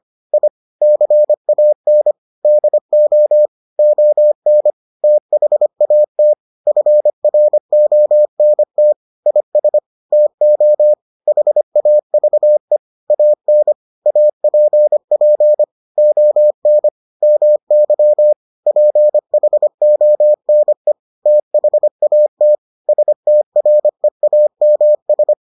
Morse code, heard as a keyed tone, about 25 wpm, 600 Hz. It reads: I CAN DO ON THAT FRONT IS TO HAVE AN APP ON MY PHONE THAT STREAMS